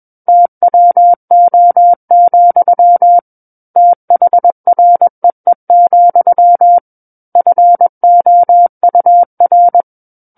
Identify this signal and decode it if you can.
Morse code, heard as a keyed tone, about 21 wpm, 700 Hz. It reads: TWO, THREE, FOUR
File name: Book One / 12-Chapter 12 - 21 WPM